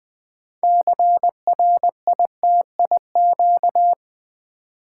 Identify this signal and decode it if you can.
Morse code, heard as a keyed tone, about 20 wpm, 700 Hz. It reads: CRITIQ
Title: Build A Word - 2 Letters At A Time - 8-Letter Words - 20wpm